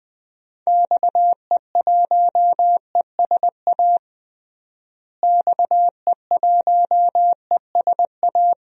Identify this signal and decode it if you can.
Morse code, heard as a keyed tone, about 20 wpm, 700 Hz. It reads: XE1ESA XE1ESA
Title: DXCC-100 Call Signs - 6 Characters - Repeated 2x - 20wpm